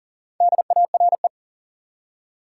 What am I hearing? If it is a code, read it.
Morse code, heard as a keyed tone, about 40 wpm, 700 Hz. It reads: DARE
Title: Four Letter Words - 40wpm